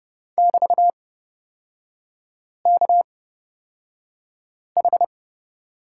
Morse code, heard as a keyed tone, about 30 wpm, spaced as for 5 wpm, 700 Hz. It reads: <BT>KH
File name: Lesson 36e - BT - Three Characters 30wpm